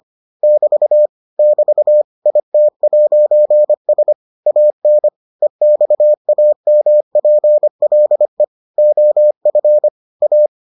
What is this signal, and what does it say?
Morse code, heard as a keyed tone, about 25 wpm, 600 Hz. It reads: <BT> <BT> IT'S AN EXAMPLE OF A